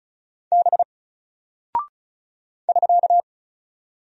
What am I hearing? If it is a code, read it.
Morse code, heard as a keyed tone, about 36 wpm, 700 Hz. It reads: B <SK>
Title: Single Character (Letter:Number:Punctuation:Prosign) Alt 35wpm